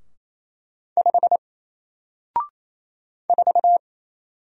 Morse code, heard as a keyed tone, about 28 wpm, 700 Hz. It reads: 5 4
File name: Single-Letter-Number-28wpm